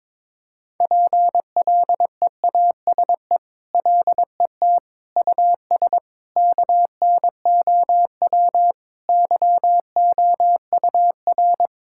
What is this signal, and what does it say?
Morse code, heard as a keyed tone, about 22 wpm, 700 Hz. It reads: PLEASE LET US KNOW YOUR
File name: practice / 22wpm / Sentences with 8-Word Phrases - 22wpm